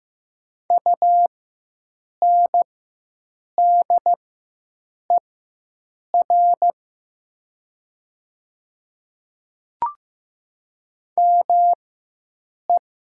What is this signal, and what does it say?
Morse code, heard as a keyed tone, about 15 wpm, spaced as for 7 wpm, 700 Hz. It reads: UNDER ME
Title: 4x Character Spacing - Top 300 Words - 15wpm